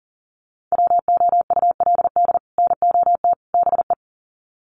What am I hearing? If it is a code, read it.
Morse code, heard as a keyed tone, about 40 wpm, 700 Hz. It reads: WOULD NOT BE
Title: Sets of 3 Words 40wpm